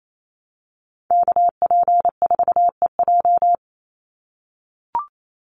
Morse code, heard as a keyed tone, about 28 wpm, 700 Hz. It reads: KP4EJ
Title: Call Signs - 28wpm